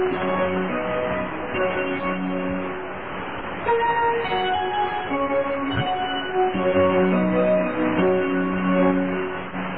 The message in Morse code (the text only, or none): none